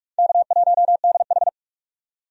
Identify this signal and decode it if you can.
Morse code, heard as a keyed tone, about 47 wpm, 700 Hz. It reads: K1DH
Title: USA Call Signs - 4 Characters - 45wpm